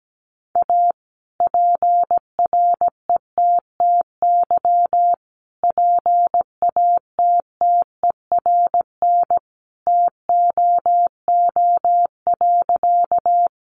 Morse code, heard as a keyed tone, about 17 wpm, 700 Hz. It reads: A PRETTY PATTERN TOO.
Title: Sentences from Top 1000 Words 17wpm